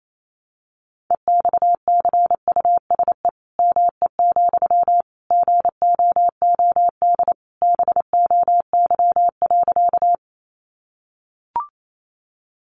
Morse code, heard as a keyed tone, about 28 wpm, 700 Hz. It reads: EXCUSE ME, GOOD BOY.
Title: Sentences from Top 700 Words 28wpm